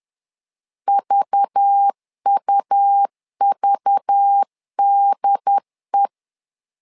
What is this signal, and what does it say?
Morse code, heard as a keyed tone, about 10 wpm, 800 Hz. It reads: VUVDE
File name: LEZIONI / 5 LBVC / 5 LBV 10wpm 2248